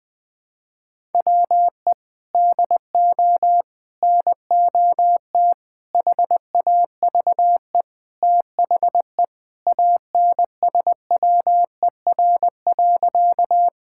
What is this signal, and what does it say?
Morse code, heard as a keyed tone, about 20 wpm, 700 Hz. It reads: WE DO NOT HAVE THE ANSWER.
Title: Sentences with 4-Word Phrases - Spoken Beginning - 20wpm